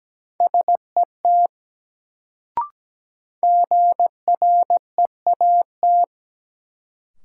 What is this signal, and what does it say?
Morse code, heard as a keyed tone, about 17 wpm, 700 Hz. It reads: SET GREAT
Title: Sentences from Top 200 Words - New Word Drill - 17wpm